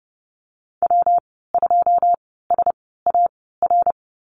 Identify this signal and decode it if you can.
Morse code, heard as a keyed tone, about 30 wpm, spaced as for 17 wpm, 700 Hz. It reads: W2SAR
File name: Lesson 19g - Letter w - Call Signs 30wpm